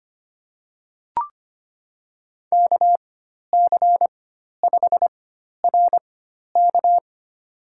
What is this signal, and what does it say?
Morse code, heard as a keyed tone, about 25 wpm, spaced as for 12 wpm, 700 Hz. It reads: KC5RK
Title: USA Call Signs - 4x Character Spacing - 25wpm